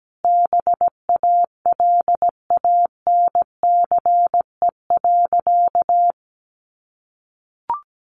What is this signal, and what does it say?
Morse code, heard as a keyed tone, about 17 wpm, 700 Hz. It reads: BALANCE.
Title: Sentences from Top 900 Words 17wpm